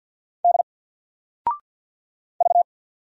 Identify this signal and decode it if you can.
Morse code, heard as a keyed tone, about 53 wpm, 700 Hz. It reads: D V
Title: Single Letter-Number 50wpm